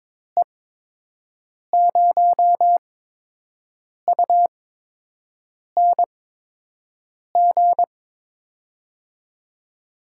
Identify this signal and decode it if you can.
Morse code, heard as a keyed tone, about 22 wpm, spaced as for 6 wpm, 700 Hz. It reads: E0UNG
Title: USA Call Signs - 6 Characters - 8x Character Spacing - 22wpm